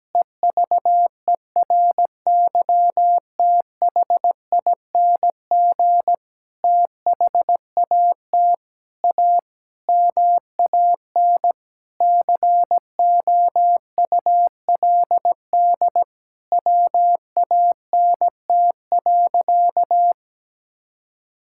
Morse code, heard as a keyed tone, about 17 wpm, 700 Hz. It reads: EVERYTHING THAT A MAN COULD WANT.
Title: Sentences from Top 1000 Words - Review Part 5 - 17wpm